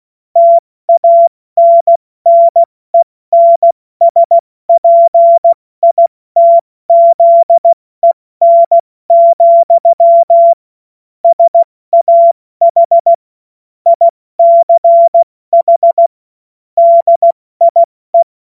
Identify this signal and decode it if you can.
Morse code, heard as a keyed tone, about 16 wpm, 680 Hz. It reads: TANNENSPITZEN, SAH ICH DIE
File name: Morsekurs / Witze / Witz 14. März 2023 70 F80